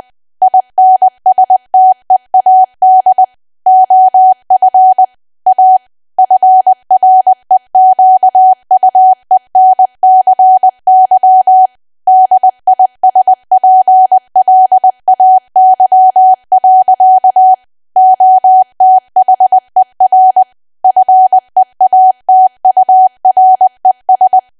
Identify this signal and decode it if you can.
Morse code, heard as a keyed tone, about 20 wpm, 750 Hz. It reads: INSTEAD OF A FREQUENCY DISPLAY. OTHER FEATURES
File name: Archive / 20 WPM / 20WPM